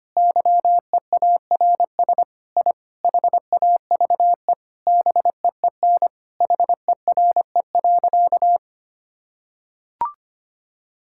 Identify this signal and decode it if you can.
Morse code, heard as a keyed tone, about 25 wpm, 700 Hz. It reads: YEARS I HAVE BEEN HERE.
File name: practice / 25wpm / Sentences with 4-Word Phrases - 25wpm